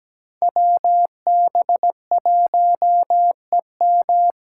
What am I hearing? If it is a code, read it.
Morse code, heard as a keyed tone, about 17 wpm, 700 Hz. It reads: WB1EM
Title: Code Groups - 5 Letter:Numbers - 17wpm